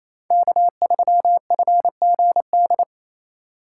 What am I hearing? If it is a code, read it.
Morse code, heard as a keyed tone, about 28 wpm, 700 Hz. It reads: K3FGD
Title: USA Call Signs - 3x Character Spacing - 28wpm